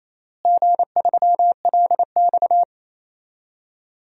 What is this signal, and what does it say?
Morse code, heard as a keyed tone, about 28 wpm, 700 Hz. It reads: G3LX
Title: DXCC-100 Call Signs - 4 Characters - Repeated 3x - 28wpm